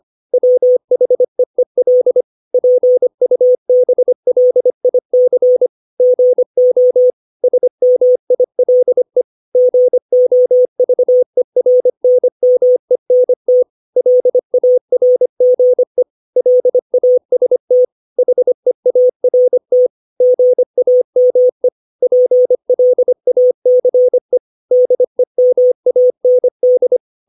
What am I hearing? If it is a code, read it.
Morse code, heard as a keyed tone, about 25 wpm, 500 Hz. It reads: WHEEL PUBLIC GO SMILE GOVERNMENT LARGE LAST HEART GAME PLACE DEMAND
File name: 632 words / 632 Words 25WPM